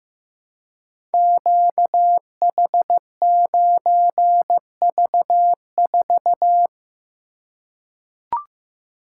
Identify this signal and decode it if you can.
Morse code, heard as a keyed tone, about 15 wpm, 700 Hz. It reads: QH9V4
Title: Code Groups - 5 Letter:Numbers - 15wpm